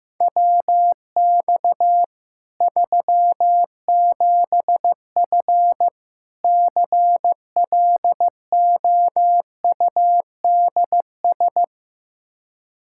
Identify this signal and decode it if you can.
Morse code, heard as a keyed tone, about 14 wpm, 700 Hz. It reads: WX 37F CLOUDS
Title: Practice QSO - Exchange 2 - Info - 15wpm